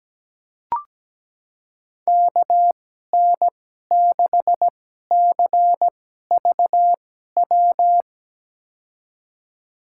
Morse code, heard as a keyed tone, about 17 wpm, spaced as for 12 wpm, 700 Hz. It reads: KN6CVW
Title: USA Call Signs - 6 Characters - 2x Character Spacing - 17wpm